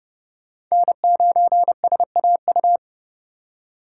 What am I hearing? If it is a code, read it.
Morse code, heard as a keyed tone, about 37 wpm, 700 Hz. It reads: N9SAU
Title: Speed Racing - USA Call Signs 30wpm